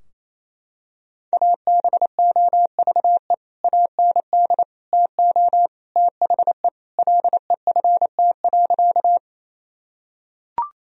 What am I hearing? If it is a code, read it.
Morse code, heard as a keyed tone, about 28 wpm, 700 Hz. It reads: ABOVE AND TO THE LEFT.
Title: Sentences from Top 1000 Words - Review Part 2 - 28wpm